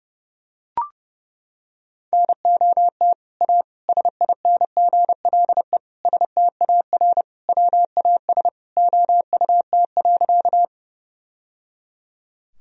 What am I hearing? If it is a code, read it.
Morse code, heard as a keyed tone, about 30 wpm, 700 Hz. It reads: NOT A SINGLE STAR WAS OUT.
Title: Sentences from Top 1000 Words - Review Part 4 - 30wpm